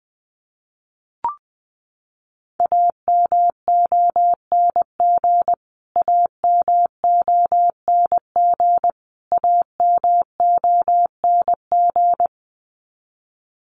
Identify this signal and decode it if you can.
Morse code, heard as a keyed tone, about 20 wpm, 700 Hz. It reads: AMONG AMONG AMONG
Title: Top 500 Words Repeated 20wpm